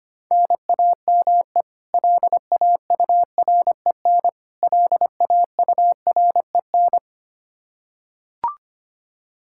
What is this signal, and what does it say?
Morse code, heard as a keyed tone, about 25 wpm, 700 Hz. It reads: NAME LAUREN LAUREN
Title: Practice QSO - Exchange 1 - Beginning - 25wpm